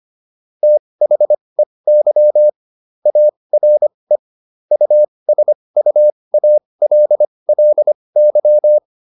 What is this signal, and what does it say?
Morse code, heard as a keyed tone, about 25 wpm, spaced as for 20 wpm, 600 Hz. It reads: THEY ARE USUALLY